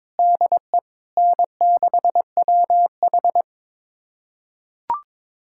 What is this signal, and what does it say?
Morse code, heard as a keyed tone, about 22 wpm, 700 Hz. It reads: DE N6WH
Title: USA Callsign DE Callsign - 4 Characters - 22wpm